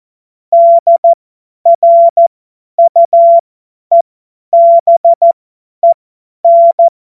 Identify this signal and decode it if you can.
Morse code, heard as a keyed tone, about 14 wpm, spaced as for 10 wpm, 680 Hz. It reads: DRUEBEN
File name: Morsekurs / Witze / Witz 07. Dezember 2021, 50 F70